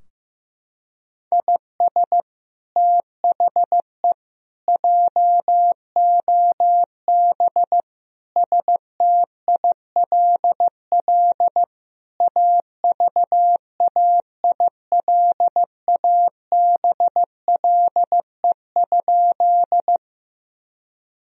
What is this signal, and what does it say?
Morse code, heard as a keyed tone, about 16 wpm, 700 Hz. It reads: IS THE JOB STILL AVAILABLE?
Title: Sentences from Top 1000 Words - Review Part 3 - 15wpm